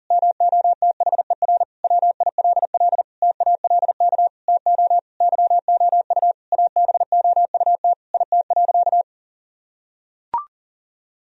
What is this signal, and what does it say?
Morse code, heard as a keyed tone, about 40 wpm, 700 Hz. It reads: MOTHER WILL TALK TO YOU ABOUT IT.
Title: Sentences from Top 1000 Words - Review Part 1 - 40wpm